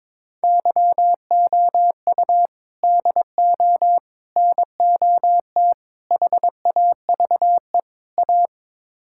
Morse code, heard as a keyed tone, about 21 wpm, 700 Hz. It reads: YOU DO NOT HAVE A
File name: practice / 22wpm / Build A Phrase - 5-Word Phrases - 22wpm